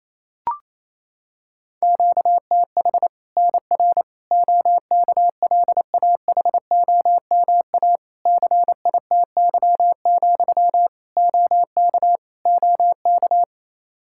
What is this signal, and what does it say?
Morse code, heard as a keyed tone, about 28 wpm, 700 Hz. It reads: QTH NR OKLAHOMA CITY, OK OK
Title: Practice QSO - Exchange 1 - Beginning - 28wpm